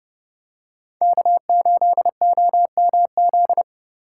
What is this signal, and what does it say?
Morse code, heard as a keyed tone, about 31 wpm, 700 Hz. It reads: K8OMZ